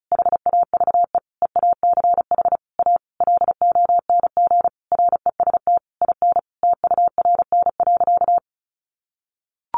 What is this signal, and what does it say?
Morse code, heard as a keyed tone, about 34 wpm, 700 Hz. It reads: HAVE EACH A LONG REST IN TURN.
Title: Sentences from Top 700 Words 35wpm